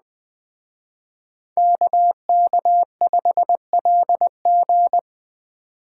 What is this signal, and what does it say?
Morse code, heard as a keyed tone, about 21 wpm, 700 Hz. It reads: KK5LG